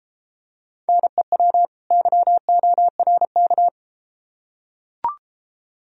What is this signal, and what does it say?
Morse code, heard as a keyed tone, about 33 wpm, 700 Hz. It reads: NEW YORK